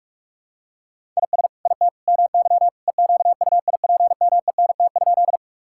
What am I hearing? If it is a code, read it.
Morse code, heard as a keyed tone, about 46 wpm, 700 Hz. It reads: IS IT MY EQUIPMENT?
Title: Sentences from Top 1000 Words - Review Part 5 - 45wpm